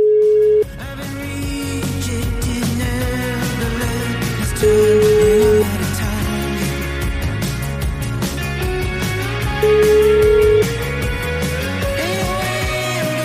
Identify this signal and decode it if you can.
Ringback tone (Europe, 425 Hz), with noise.